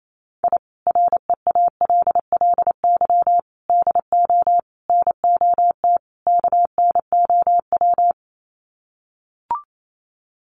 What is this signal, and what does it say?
Morse code, heard as a keyed tone, about 28 wpm, 700 Hz. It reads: I REALLY DO NOT KNOW
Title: Sets of 5 Words Encore 28wpm